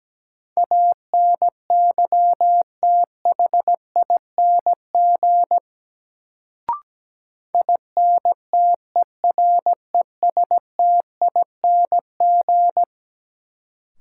Morse code, heard as a keyed tone, about 17 wpm, 700 Hz. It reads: ANYTHING INTERESTING
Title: Sentences from Top 1000 Words - New Word Drill - 17wpm